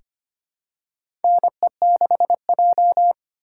Morse code, heard as a keyed tone, about 25 wpm, 700 Hz. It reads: NE6J
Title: USA Call Signs - 3x Character Spacing - 25wpm